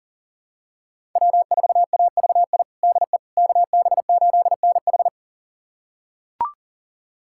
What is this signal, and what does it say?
Morse code, heard as a keyed tone, about 40 wpm, 700 Hz. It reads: W4AVI DE KB8NH
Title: USA Callsign DE Callsign - 5 Characters - 40wpm